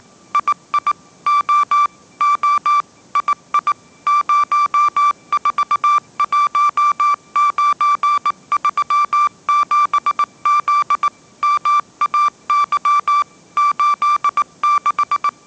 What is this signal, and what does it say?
Morse code, heard as a keyed tone, about 18 wpm, 1170 Hz. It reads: II O O II 041937Z MAY 86